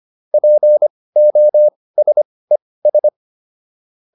Morse code, heard as a keyed tone, about 25 wpm, spaced as for 18 wpm, 600 Hz. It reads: POSES